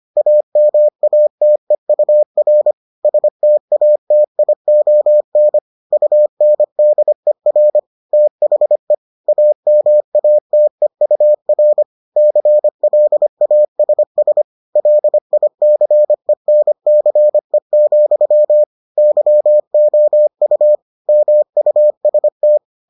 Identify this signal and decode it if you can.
Morse code, heard as a keyed tone, about 25 wpm, 600 Hz. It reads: AMATEUR STATION UNDER THE AMATEUR CLASS LICENCE, YOU MUST